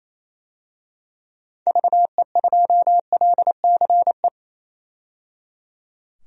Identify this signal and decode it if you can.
Morse code, heard as a keyed tone, about 28 wpm, 700 Hz. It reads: VE2LCE
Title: DXCC-100 Call Signs - 6 Characters - 28wpm